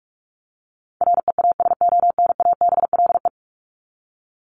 Morse code, heard as a keyed tone, about 44 wpm, 700 Hz. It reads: REASONABLE
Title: Top 1501-1750 Words 45wpm